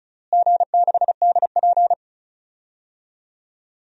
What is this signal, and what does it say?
Morse code, heard as a keyed tone, about 36 wpm, 700 Hz. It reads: G6DP
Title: DXCC-100 Call Signs - 4 Characters - Repeated 2x - 35wpm